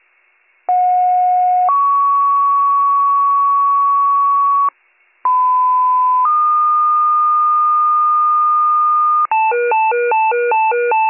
Two-tone page, heard as a one-second tone, then a three-second tone, about 730 then 1120 Hz, about 1010 then 1220 Hz. It is heard from 0.7 to 9.3 s.